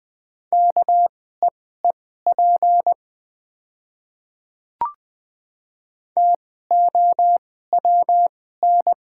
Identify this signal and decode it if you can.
Morse code, heard as a keyed tone, about 20 wpm, spaced as for 15 wpm, 700 Hz. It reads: KEEP TOWN
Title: 2x Character Spacing - Top 500 Words - 20wpm